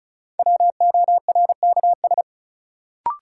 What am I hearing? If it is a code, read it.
Morse code, heard as a keyed tone, about 35 wpm, 700 Hz. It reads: WORKS
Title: Sentences from Top 1000 Words - Review Part 5 - New Word Drill - 35wpm